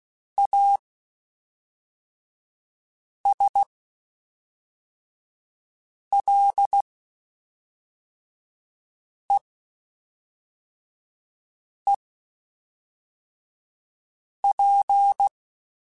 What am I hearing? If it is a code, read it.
Morse code, heard as a keyed tone, about 16 wpm, spaced as for 3 wpm, 800 Hz. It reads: ASLEEP
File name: Quotes Spaced / Quotes Spaced-mp3s / ChildAsleep sp @5